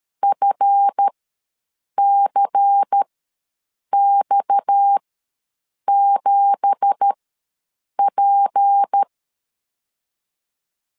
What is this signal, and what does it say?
Morse code, heard as a keyed tone, about 13 wpm, spaced as for 7 wpm, 800 Hz. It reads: FCX7P
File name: LEZIONI / 9 2867 ws 12wpm 3431